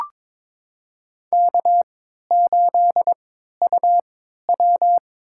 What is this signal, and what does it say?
Morse code, heard as a keyed tone, about 22 wpm, spaced as for 13 wpm, 700 Hz. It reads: K8UW
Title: USA Call Signs - 3x Character Spacing - 22wpm